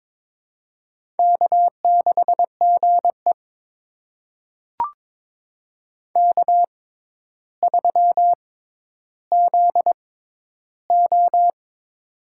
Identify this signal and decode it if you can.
Morse code, heard as a keyed tone, about 22 wpm, spaced as for 11 wpm, 700 Hz. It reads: K6GE K3ZO
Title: USA Call Signs - 4 Characters - 6x Character Spacing - 22wpm